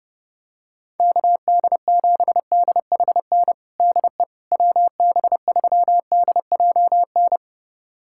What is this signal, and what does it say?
Morse code, heard as a keyed tone, about 30 wpm, 700 Hz. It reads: KD7DHN DE WB3DJN
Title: USA Callsign DE Callsign 30wpm